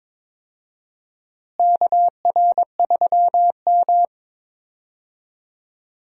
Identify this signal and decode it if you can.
Morse code, heard as a keyed tone, about 22 wpm, 700 Hz. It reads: KR3M
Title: USA Call Signs - 4 Characters - Repeated 3x - 22wpm